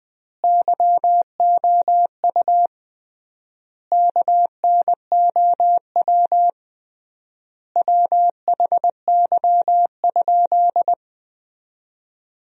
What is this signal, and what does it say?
Morse code, heard as a keyed tone, about 20 wpm, 700 Hz. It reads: YOU KNOW WHY?
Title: 3x Word-Spacing - Sentences from Top 200 Words - 20wpm